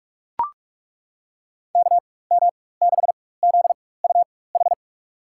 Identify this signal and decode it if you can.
Morse code, heard as a keyed tone, about 43 wpm, spaced as for 21 wpm, 700 Hz. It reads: KM6ZUH
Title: USA Call Signs - 6 Characters - 4x Character Spacing - 45wpm